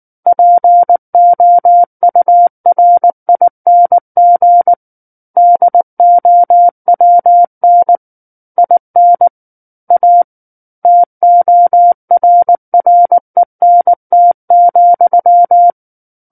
Morse code, heard as a keyed tone, about 19 wpm, 700 Hz. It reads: POURING DOWN IN A TORRENT,
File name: Book One / 10-Chapter 10 - 19 WPM